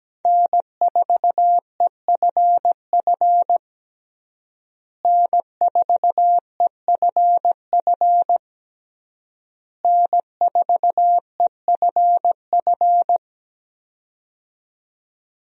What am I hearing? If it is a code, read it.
Morse code, heard as a keyed tone, about 17 wpm, 700 Hz. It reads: N4EFF N4EFF N4EFF
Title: USA Call Signs - 5 Characters - Repeated 3x - 17wpm